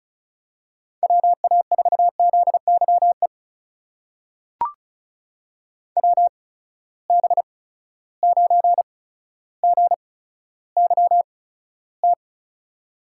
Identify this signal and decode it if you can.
Morse code, heard as a keyed tone, about 36 wpm, spaced as for 15 wpm, 700 Hz. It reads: WA4ZYE WB9GYT
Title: USA Call Signs - 6 Characters - 8x Character Spacing - 35wpm